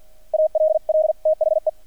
Morse code, heard as a keyed tone, about 50 wpm, 630 Hz. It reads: ALL THE
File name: archived / CRSnet